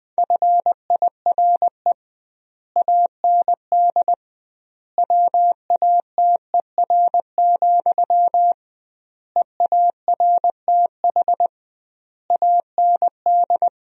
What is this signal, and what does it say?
Morse code, heard as a keyed tone, about 20 wpm, 700 Hz. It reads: FIRE AND WATER, EARTH AND
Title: 2x Word-Spacing - Sentences from Top 400 Words - 20wpm